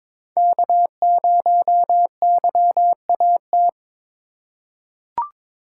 Morse code, heard as a keyed tone, about 22 wpm, 700 Hz. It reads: K0YAT
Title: USA Call Signs - 5 Characters - 2x Character Spacing - 22wpm